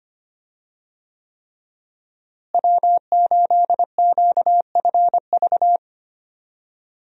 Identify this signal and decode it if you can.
Morse code, heard as a keyed tone, about 25 wpm, 700 Hz. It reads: W8QFV
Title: USA Call Signs - 5 Characters - Repeated 3x - 25wpm